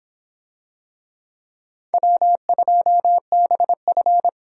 Morse code, heard as a keyed tone, about 25 wpm, 700 Hz. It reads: W2BF